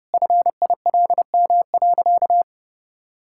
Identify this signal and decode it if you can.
Morse code, heard as a keyed tone, about 28 wpm, 700 Hz. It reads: FILM.
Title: Sentences from Top 1000 Words - Review Part 5 - 30wpm